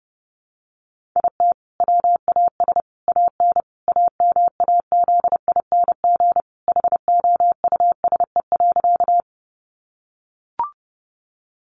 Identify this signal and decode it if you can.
Morse code, heard as a keyed tone, about 30 wpm, 700 Hz. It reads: IT WAS AN AMAZING HOUSE.
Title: Sentences from Top 1500 Words 30wpm